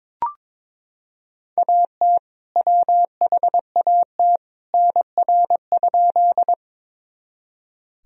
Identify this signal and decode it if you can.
Morse code, heard as a keyed tone, about 21 wpm, 700 Hz. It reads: AT WHAT NR?
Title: Sentences from Top 100 Words and Abbreviations 22wpm